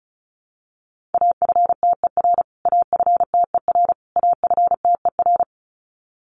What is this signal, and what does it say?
Morse code, heard as a keyed tone, about 34 wpm, 700 Hz. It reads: AFTER AFTER AFTER
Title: Top 200 Words Repeated 35wpm